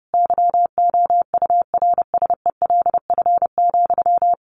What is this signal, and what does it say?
Morse code, heard as a keyed tone, about 30 wpm, 700 Hz. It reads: YOURSELF,
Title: Sentences from Top 1100 Words 30wpm